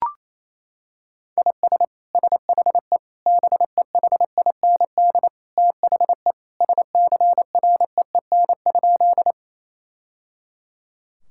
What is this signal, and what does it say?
Morse code, heard as a keyed tone, about 28 wpm, 700 Hz. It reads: IS SHE BEHIND THE SCREEN?
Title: Sentences from Top 1000 Words - Review Part 6 - 28wpm